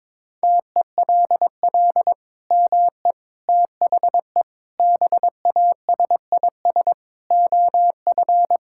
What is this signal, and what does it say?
Morse code, heard as a keyed tone, about 22 wpm, 700 Hz. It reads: TELL ME THE BASIS OF
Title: Sentences from Top 1000 Words - Review Part 5 - 22wpm